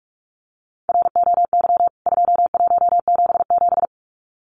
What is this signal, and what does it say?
Morse code, heard as a keyed tone, about 44 wpm, 700 Hz. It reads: ROY 2177
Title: CWT Exchange 45wpm